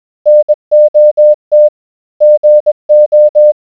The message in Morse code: NOT GO